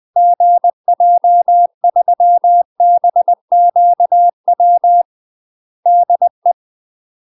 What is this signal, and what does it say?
Morse code, heard as a keyed tone, about 20 wpm, 700 Hz. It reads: GJ3BQW DE
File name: UK MASTER QSOs TRAINING / 20WPM / UK MASTER QSOs 0025